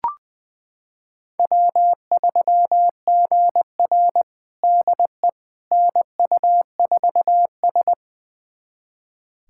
Morse code, heard as a keyed tone, about 20 wpm, 700 Hz. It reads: W3GR DE NU4S
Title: USA Callsign DE Callsign - 4 Characters - 20wpm